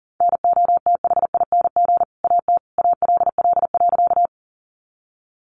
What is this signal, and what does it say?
Morse code, heard as a keyed tone, about 37 wpm, 700 Hz. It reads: NOTHING AT ALL.
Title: Sentences from Top 500 Words 40wpm